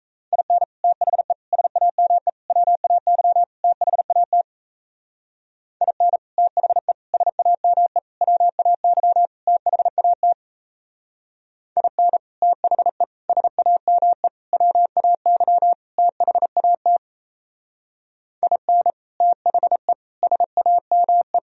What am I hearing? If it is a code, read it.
Morse code, heard as a keyed tone, about 36 wpm, 700 Hz. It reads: IN THE SAME WAY THAT IN THE SAME WAY THAT IN THE SAME WAY THAT IN THE SAME